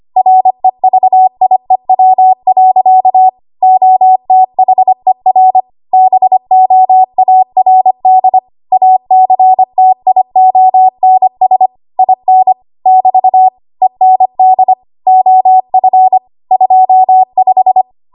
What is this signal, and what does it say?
Morse code, heard as a keyed tone, about 25 wpm, 750 Hz. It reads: REVIEW. OTHER BOARD ACTIONS IN <BT> END OF 25
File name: Archive / 25 WPM / 25WPM